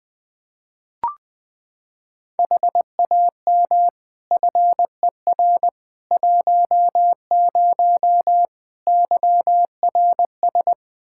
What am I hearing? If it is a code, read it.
Morse code, heard as a keyed tone, about 20 wpm, 700 Hz. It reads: HAM FER 10 YRS
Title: Practice QSO - Exchange 3 - More Info - 20wpm